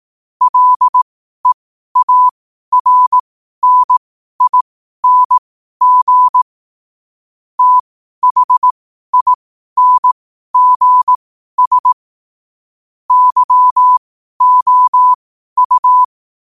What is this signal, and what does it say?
Morse code, heard as a keyed tone, about 18 wpm, spaced as for 12 wpm, 1000 Hz. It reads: LEARNING THINGS YOU
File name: qotdfiles / 12wpm